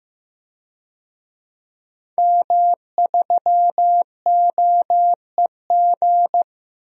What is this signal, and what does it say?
Morse code, heard as a keyed tone, about 15 wpm, 700 Hz. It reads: M3OEG
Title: DXCC-100 Call Signs - 5 Characters - 15wpm